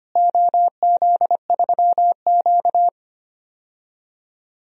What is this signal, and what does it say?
Morse code, heard as a keyed tone, about 25 wpm, 700 Hz. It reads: OZ3Q
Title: DXCC-100 Call Signs - 4 Characters - Repeated 3x - 25wpm